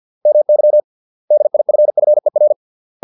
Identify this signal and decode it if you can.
Morse code, heard as a keyed tone, about 50 wpm, 600 Hz. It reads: MY DIFFER